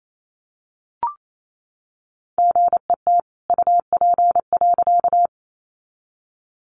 Morse code, heard as a keyed tone, about 28 wpm, 700 Hz. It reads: GET UP.
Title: A Sound Of Thunder 28wpm